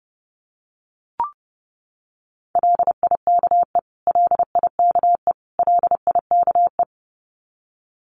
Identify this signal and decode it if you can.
Morse code, heard as a keyed tone, about 30 wpm, 700 Hz. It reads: LIKE LIKE LIKE
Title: Top 500 Words Repeated 30wpm